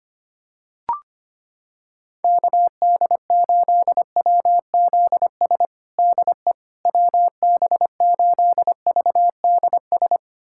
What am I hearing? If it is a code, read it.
Morse code, heard as a keyed tone, about 25 wpm, 700 Hz. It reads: KD8WZS DE WB8VDS
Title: USA Callsign DE Callsign - 6 Characters - 25wpm